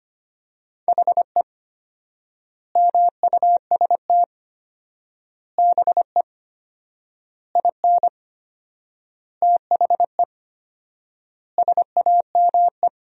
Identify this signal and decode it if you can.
Morse code, heard as a keyed tone, about 25 wpm, 700 Hz. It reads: HE MUST BE IN THE SAME
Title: 4x Word-Spacing - Sentences from Top 500 Words - 25wpm